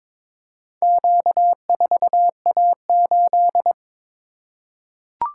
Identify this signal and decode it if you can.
Morse code, heard as a keyed tone, about 22 wpm, 700 Hz. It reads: Q4A8
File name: practice / 22wpm / Four Letter-Number Combinations - 22wpm